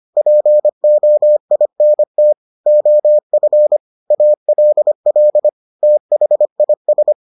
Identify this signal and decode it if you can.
Morse code, heard as a keyed tone, about 25 wpm, 600 Hz. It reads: POINT OF ALL THIS